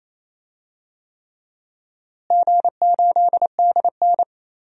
Morse code, heard as a keyed tone, about 28 wpm, 700 Hz. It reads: G8DN
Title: DXCC-100 Call Signs - 4 Characters - Repeated 3x - 28wpm